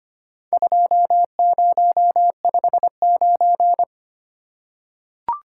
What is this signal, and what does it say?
Morse code, heard as a keyed tone, about 25 wpm, 690 Hz. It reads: 2059
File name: practice / 25wpm / Four Letter-Number Combinations - 25wpm